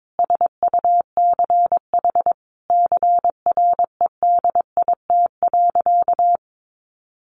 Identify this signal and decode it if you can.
Morse code, heard as a keyed tone, about 22 wpm, 700 Hz. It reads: SUCH CREDIT.
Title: Sentences from Top 1000 Words 22wpm